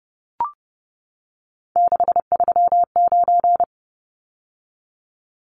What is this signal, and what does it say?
Morse code, heard as a keyed tone, about 30 wpm, 700 Hz. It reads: 639
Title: Numbers - 3-Digit 30wpm